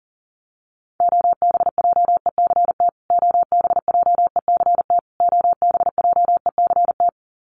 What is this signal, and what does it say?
Morse code, heard as a keyed tone, about 40 wpm, 700 Hz. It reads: OBJECT OBJECT OBJECT
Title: Top 500 Words Repeated 40wpm